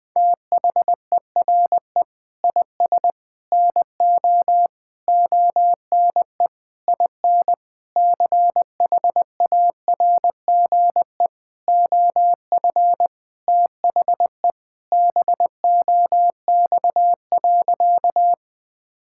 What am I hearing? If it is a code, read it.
Morse code, heard as a keyed tone, about 21 wpm, 700 Hz. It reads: THERE IS NO ONE IN CHARGE OF THE BOX.
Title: Sentences from Top 1000 Words - Review Part 6 - 20wpm